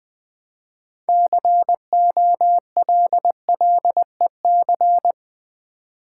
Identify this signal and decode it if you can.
Morse code, heard as a keyed tone, about 20 wpm, 700 Hz. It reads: COLLEC
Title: Build A Word - 3 Letters At A Time - 9-Letter Words - 20wpm